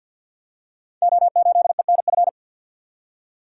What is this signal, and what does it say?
Morse code, heard as a keyed tone, about 49 wpm, 700 Hz. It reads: O8ENF
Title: Code Groups - 5 Letter:Numbers - 50wpm